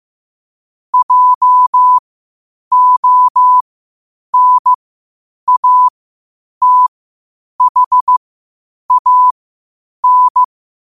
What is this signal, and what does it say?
Morse code, heard as a keyed tone, about 15 wpm, spaced as for 8 wpm, 1000 Hz. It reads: JONATHAN
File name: qotdfiles / 8wpm